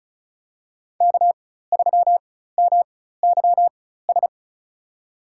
Morse code, heard as a keyed tone, about 36 wpm, spaced as for 16 wpm, 700 Hz. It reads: K3MYS
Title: USA Call Signs - 5 Characters - 4x Character Spacing - 35wpm